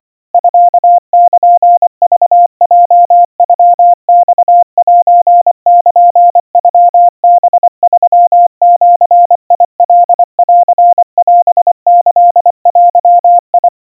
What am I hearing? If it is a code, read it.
Morse code (Japanese Wabun code), heard as a keyed tone, about 24 wpm, 700 Hz. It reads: ミルクヲノマセルノハ3ジカンオキデ
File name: 90 cpm 700 Hz Crispy